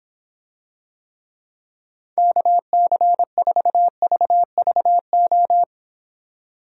Morse code, heard as a keyed tone, about 26 wpm, 700 Hz. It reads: KC4VVO